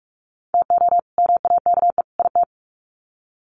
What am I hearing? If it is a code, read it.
Morse code, heard as a keyed tone, about 44 wpm, 700 Hz. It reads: TO MAKE IT
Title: Sets of 3 Words 45wpm